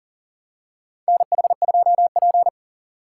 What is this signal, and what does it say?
Morse code, heard as a keyed tone, about 40 wpm, 700 Hz. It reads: NH2P
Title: USA Call Signs - 4 Characters - 2x Character Spacing - 40wpm